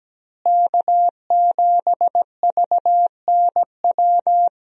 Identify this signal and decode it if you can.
Morse code, heard as a keyed tone, about 17 wpm, 700 Hz. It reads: K7VNW
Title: USA Call Signs - 5 Characters - Repeated 2x - 17wpm